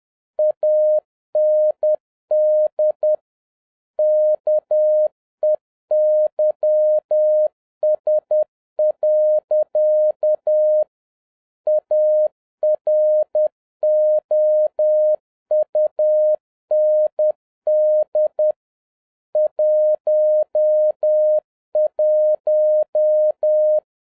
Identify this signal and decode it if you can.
Morse code, heard as a keyed tone, about 10 wpm, 610 Hz. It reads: AND KEYS. AROUND 11
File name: 2011-07-mp3 / n46 JOHN K3TN 10wpm